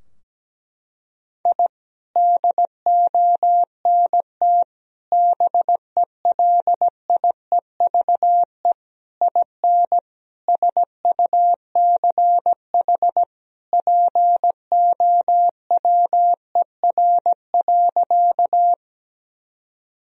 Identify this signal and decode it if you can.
Morse code, heard as a keyed tone, about 17 wpm, 700 Hz. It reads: I DONT BELIEVE IN SUCH POWER.
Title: Sentences from Top 1000 Words - Review Part 2 - 17wpm